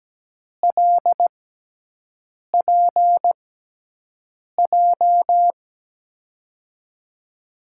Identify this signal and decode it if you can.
Morse code, heard as a keyed tone, about 17 wpm, spaced as for 6 wpm, 700 Hz. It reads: LPJ
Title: USA Call Signs - 6 Characters - 6x Character Spacing - 17wpm